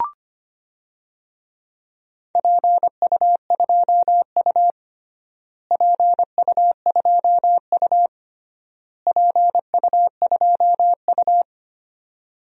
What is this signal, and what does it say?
Morse code, heard as a keyed tone, about 25 wpm, 700 Hz. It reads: PU2U PU2U PU2U
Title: DXCC-100 Call Signs - 4 Characters - Repeated 3x - 25wpm